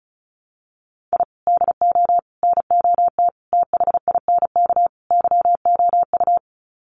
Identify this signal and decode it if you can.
Morse code, heard as a keyed tone, about 34 wpm, 700 Hz. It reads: I DO NOT THINK YOU
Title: Sets of 5 Words 35wpm